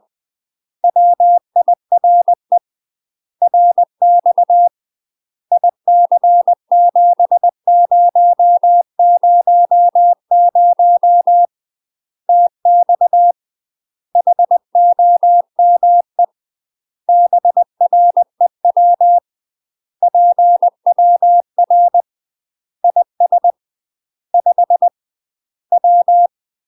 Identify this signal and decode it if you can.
Morse code, heard as a keyed tone, about 20 wpm, 700 Hz. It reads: WIRE RX IC7000 TX HOME BREW PWR IS 5 W
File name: UK MASTER QSOs TRAINING / 20WPM / UK MASTER QSOs 0002